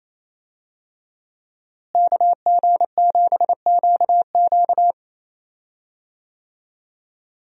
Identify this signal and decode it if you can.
Morse code, heard as a keyed tone, about 28 wpm, 700 Hz. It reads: KG7QQ
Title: USA Call Signs - 5 Characters - Repeated 2x - 28wpm